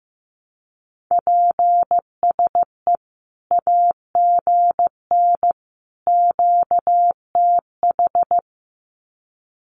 Morse code, heard as a keyed tone, about 15 wpm, 700 Hz. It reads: PSE AGN QTH
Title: Common QSO Phrases 15wpm